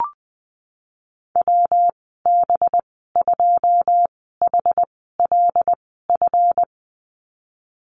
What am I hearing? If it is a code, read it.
Morse code, heard as a keyed tone, about 20 wpm, spaced as for 15 wpm, 700 Hz. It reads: WB2HLF
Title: 186 Lesson 30g - Letter b - Call Signs 20wpm